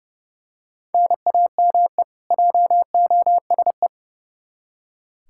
Morse code, heard as a keyed tone, about 30 wpm, 700 Hz. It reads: NAME JOSE
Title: Practice QSO - Exchange 1 - Beginning - 30wpm